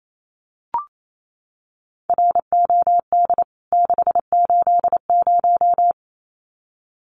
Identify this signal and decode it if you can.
Morse code, heard as a keyed tone, about 28 wpm, 700 Hz. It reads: ROD 680
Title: CWT Exchange 28wpm